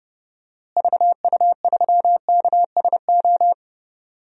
Morse code, heard as a keyed tone, about 30 wpm, 700 Hz. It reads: VU3KSO
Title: DXCC-100 Call Signs - 6 Characters - Repeated 2x - 30wpm